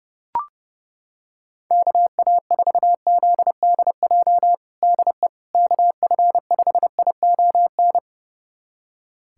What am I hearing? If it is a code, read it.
Morse code, heard as a keyed tone, about 32 wpm, 700 Hz. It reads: KA4ZDJ DE KF5ION
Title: USA Callsign DE Callsign - 6 Characters - 30wpm